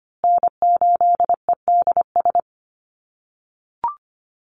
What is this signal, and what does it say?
Morse code, heard as a keyed tone, about 25 wpm, 700 Hz. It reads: N8EDS
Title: USA Call Signs Repeated 2x 25wpm